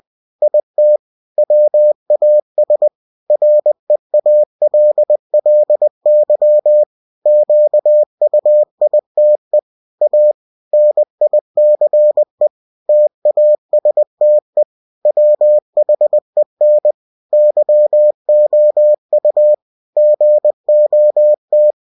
Morse code, heard as a keyed tone, about 20 wpm, 600 Hz. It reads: IT WAS REALLY QUITE A NICE TASTE WHEN YOU GOT
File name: cw-books / winnie-the-pooh / winnie-20wpm-page-0074